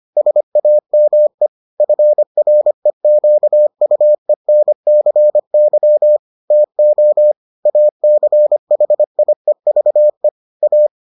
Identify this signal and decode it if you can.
Morse code, heard as a keyed tone, about 25 wpm, 600 Hz. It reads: SAME FREQUENCY TO ACHIEVE A